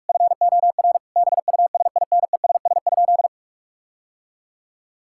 Morse code, heard as a keyed tone, about 48 wpm, 700 Hz. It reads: FOR BUSINESS?
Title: Sentences from Top 1000 Words - Review Part 6 - 45wpm